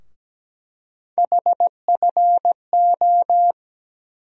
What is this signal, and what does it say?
Morse code, heard as a keyed tone, about 17 wpm, 700 Hz. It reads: HFO
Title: USA Call Sign Postfixes - 3 Letters 17wpm